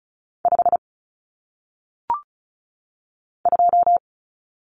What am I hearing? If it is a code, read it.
Morse code, heard as a keyed tone, about 36 wpm, 700 Hz. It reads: 5 2
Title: Numbers - 1-Digit 35wpm